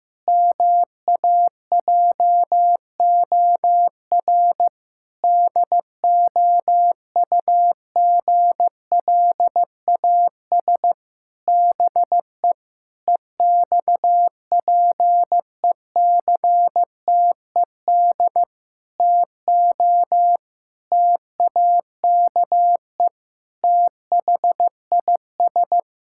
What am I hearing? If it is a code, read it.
Morse code, heard as a keyed tone, about 16 wpm, 700 Hz. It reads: MAJOR DOUGLAS BE EXPECTED TO TAKE THIS
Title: Pangram Challenge v2 - 15wpm